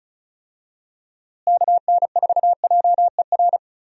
Morse code, heard as a keyed tone, about 36 wpm, 700 Hz. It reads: KN4JER
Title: USA Call Signs - 6 Characters - Repeated 2x - 35wpm